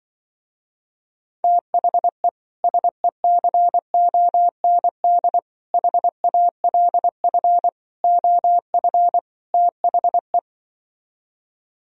Morse code, heard as a keyed tone, about 24 wpm, 700 Hz. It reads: THE SECOND HALF OF THE
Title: Speed-Racing - Sets of 5 Words 20wpm